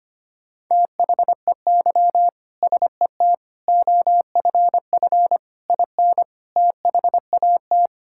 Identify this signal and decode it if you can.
Morse code, heard as a keyed tone, about 25 wpm, 700 Hz. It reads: THEY SET OFF IN THAT
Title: Sentences from Top 1000 Words - Review Part 5 - 25wpm